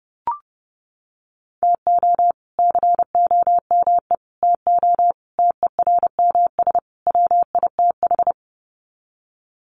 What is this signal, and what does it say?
Morse code, heard as a keyed tone, about 28 wpm, 700 Hz. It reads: TO COME TO TERMS WITH
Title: Sets of 5 Words Encore 30wpm